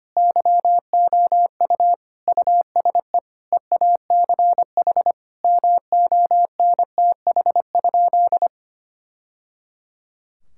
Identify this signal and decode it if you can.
Morse code, heard as a keyed tone, about 25 wpm, 700 Hz. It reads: YOU USE EACH MONTH?
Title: Sentences from Top 1000 Words - Review Part 6 - 25wpm